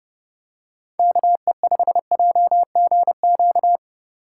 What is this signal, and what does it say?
Morse code, heard as a keyed tone, about 30 wpm, 700 Hz. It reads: KE5JGQ